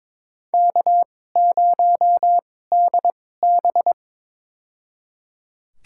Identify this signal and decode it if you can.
Morse code, heard as a keyed tone, about 22 wpm, spaced as for 16 wpm, 700 Hz. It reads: K0DB
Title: USA Call Signs - 4 Characters - 2x Character Spacing - 22wpm